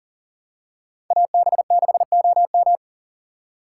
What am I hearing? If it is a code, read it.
Morse code, heard as a keyed tone, about 40 wpm, 700 Hz. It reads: AB6OM
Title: USA Call Signs - 5 Characters - 3x Character Spacing - 40wpm